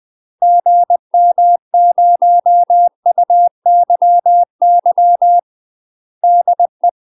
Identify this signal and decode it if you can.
Morse code, heard as a keyed tone, about 20 wpm, 700 Hz. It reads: GM0UYY DE
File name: UK MASTER QSOs TRAINING / 20WPM / UK MASTER QSOs 0049